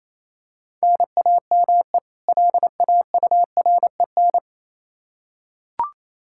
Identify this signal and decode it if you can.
Morse code, heard as a keyed tone, about 28 wpm, 700 Hz. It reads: NAME LAUREN
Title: K1USN SST QSO Practice 28wpm